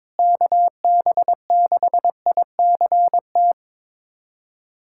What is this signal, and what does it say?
Morse code, heard as a keyed tone, about 22 wpm, 700 Hz. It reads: KB6ICT
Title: USA Callsign DE Callsign 22wpm